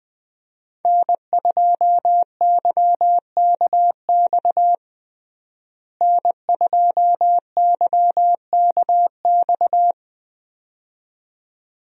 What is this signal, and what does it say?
Morse code, heard as a keyed tone, about 21 wpm, 700 Hz. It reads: N2YKX N2YKX
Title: USA Call Signs - 5 Characters - Repeated 2x - 20wpm